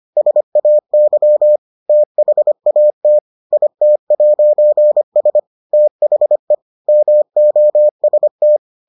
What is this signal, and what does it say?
Morse code, heard as a keyed tone, about 25 wpm, 600 Hz. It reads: SAY THAT IT'S THE MOST